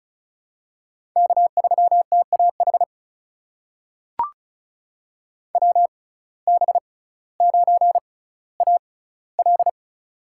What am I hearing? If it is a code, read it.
Morse code, heard as a keyed tone, about 36 wpm, spaced as for 18 wpm, 700 Hz. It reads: K3TAH WB9AL
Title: USA Call Signs - 5 Characters - 6x Character Spacing - 35wpm